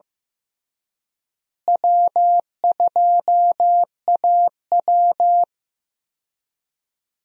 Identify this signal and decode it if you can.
Morse code, heard as a keyed tone, about 15 wpm, 700 Hz. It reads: W2AW
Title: USA Call Signs - 4 Characters - Repeated 3x - 15wpm